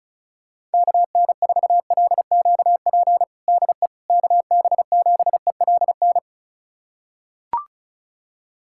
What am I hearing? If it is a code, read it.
Morse code, heard as a keyed tone, about 36 wpm, 700 Hz. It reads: KN4LQP DE KB7ELN
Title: USA Callsign DE Callsign 35wpm